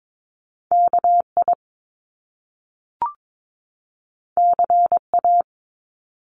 Morse code, heard as a keyed tone, about 22 wpm, 700 Hz. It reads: KI CA
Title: Top Two Letter Combinations in Words 22wpm